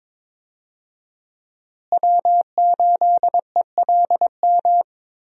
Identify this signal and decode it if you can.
Morse code, heard as a keyed tone, about 22 wpm, 700 Hz. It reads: W8ELM
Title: USA Call Signs - 5 Characters - Repeated 2x - 22wpm